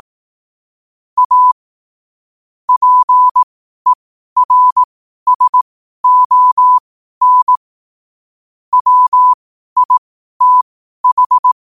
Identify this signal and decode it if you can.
Morse code, heard as a keyed tone, about 18 wpm, spaced as for 12 wpm, 1000 Hz. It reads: A PERSON WITH
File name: qotdfiles / 12wpm